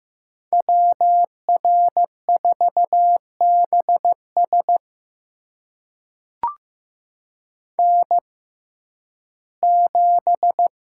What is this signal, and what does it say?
Morse code, heard as a keyed tone, about 15 wpm, spaced as for 11 wpm, 700 Hz. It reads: WR4BS N7
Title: USA Call Signs - 5 Characters - 6x Character Spacing - 15wpm